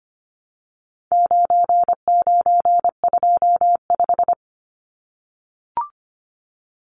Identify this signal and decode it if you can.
Morse code, heard as a keyed tone, about 25 wpm, 700 Hz. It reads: 9925
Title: Numbers - 4-Digit 25wpm